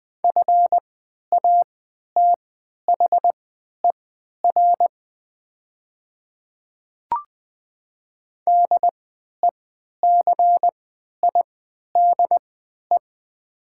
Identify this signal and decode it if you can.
Morse code, heard as a keyed tone, about 20 wpm, spaced as for 11 wpm, 690 Hz. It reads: FATHER DECIDE
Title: 3x Character Spacing - Top 500 Words - 20wpm